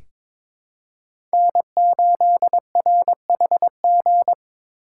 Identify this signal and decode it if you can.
Morse code, heard as a keyed tone, about 22 wpm, 700 Hz. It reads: N8RHG
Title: USA Call Signs - 5 Characters - 4x Character Spacing - 22wpm